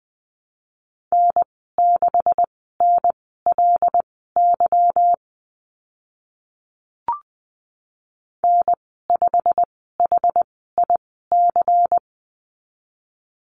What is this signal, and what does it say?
Morse code, heard as a keyed tone, about 19 wpm, spaced as for 14 wpm, 700 Hz. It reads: N6NLY N5HIC
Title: 167 Lesson 27g - Number 7 - Call Signs 20wpm